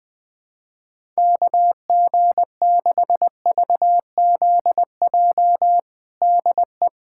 Morse code, heard as a keyed tone, about 20 wpm, 700 Hz. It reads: KG6VZJ DE
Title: USA Callsign DE Callsign - 6 Characters - 20wpm